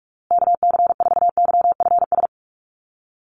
Morse code, heard as a keyed tone, about 44 wpm, 700 Hz. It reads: KC4YFS
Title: USA Call Signs Repeated 3x 45wpm